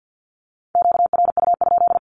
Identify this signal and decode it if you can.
Morse code, heard as a keyed tone, about 48 wpm, 700 Hz. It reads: QRU?
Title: Q-Codes 50wpm